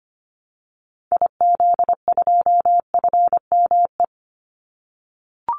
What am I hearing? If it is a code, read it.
Morse code, heard as a keyed tone, about 25 wpm, 700 Hz. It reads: IZ2FME
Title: Call Signs - 25wpm